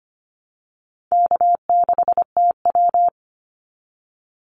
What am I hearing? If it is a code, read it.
Morse code, heard as a keyed tone, about 25 wpm, 700 Hz. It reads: K6TW
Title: NA SOTA Activator Callsigns 25wpm